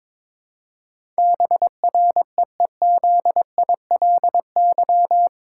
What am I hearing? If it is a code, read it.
Morse code, heard as a keyed tone, about 22 wpm, 700 Hz. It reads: BREEZILY
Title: Pangram Challenge v2 - 22wpm